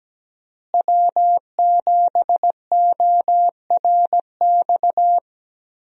Morse code, heard as a keyed tone, about 17 wpm, 700 Hz. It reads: W7ORX
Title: USA Call Signs - 5 Characters - Repeated 2x - 17wpm